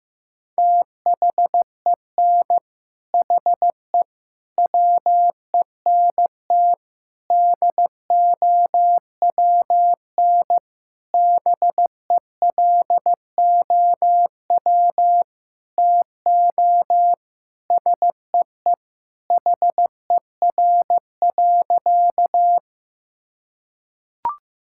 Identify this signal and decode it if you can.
Morse code, heard as a keyed tone, about 15 wpm, 700 Hz. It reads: THEN HE WENT DOWN BELOW TO SEE HER.
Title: Sentences from Top 1000 Words - Review Part 5 - 15wpm